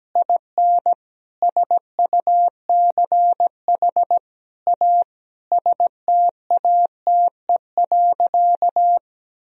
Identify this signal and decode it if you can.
Morse code, heard as a keyed tone, about 17 wpm, 700 Hz. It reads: IN SUCH A STATE.
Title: Sentences from Top 1000 Words - Review Part 2 - 17wpm